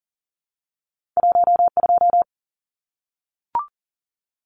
Morse code, heard as a keyed tone, about 40 wpm, 700 Hz. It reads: 12
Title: Numbers - 2-Digit 40wpm